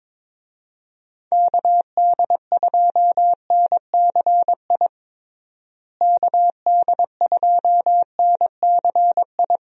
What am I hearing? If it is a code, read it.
Morse code, heard as a keyed tone, about 22 wpm, 700 Hz. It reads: KD2NCI KD2NCI
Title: USA Call Signs - 6 Characters - Repeated 2x - 22wpm